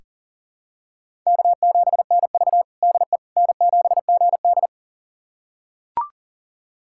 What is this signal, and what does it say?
Morse code, heard as a keyed tone, about 40 wpm, 700 Hz. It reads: K7NV DE N7GD
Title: USA Callsign DE Callsign - 4 Characters - 40wpm